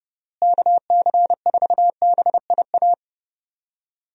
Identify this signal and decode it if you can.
Morse code, heard as a keyed tone, about 32 wpm, 700 Hz. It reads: KC4BIA
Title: USA Call Signs - 6 Characters - Repeated 2x - 30wpm